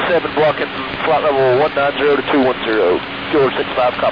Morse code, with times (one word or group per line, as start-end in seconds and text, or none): none